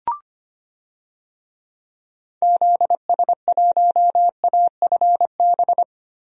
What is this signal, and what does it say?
Morse code, heard as a keyed tone, about 25 wpm, 700 Hz. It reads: ZS1AFB
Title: DXCC-100 Call Signs - 6 Characters - Repeated 3x - 25wpm